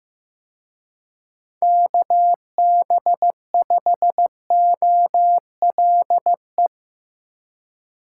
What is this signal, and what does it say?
Morse code, heard as a keyed tone, about 17 wpm, 700 Hz. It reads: KB5OLE
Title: Speed Racing - USA 2x3 Call Signs 15wpm